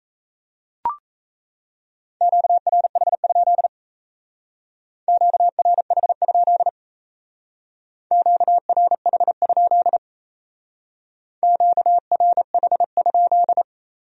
Morse code, heard as a keyed tone, about 33 wpm, 700 Hz. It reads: QRH? QRH? QRH? QRH?
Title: Speed-Racing Q-Codes - 28wpm